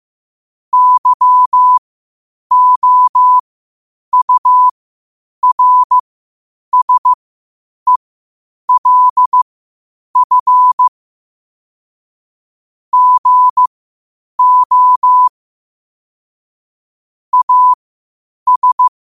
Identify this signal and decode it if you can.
Morse code, heard as a keyed tone, about 15 wpm, spaced as for 8 wpm, 1000 Hz. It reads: YOURSELF GO AS
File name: 8wpm